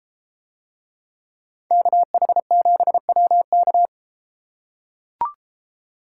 Morse code, heard as a keyed tone, about 31 wpm, 700 Hz. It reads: KH7WK